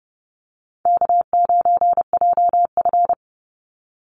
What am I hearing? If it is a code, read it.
Morse code, heard as a keyed tone, about 29 wpm, 700 Hz. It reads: K9JF
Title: Easy Call Signs Repeated 2x - 30wpm